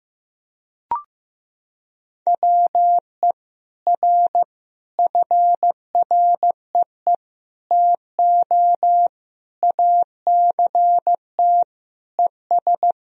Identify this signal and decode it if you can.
Morse code, heard as a keyed tone, about 16 wpm, 700 Hz. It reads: WE R FREE TO ACT ES
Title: Sentences from Top 500 Words and Abbreviations 15wpm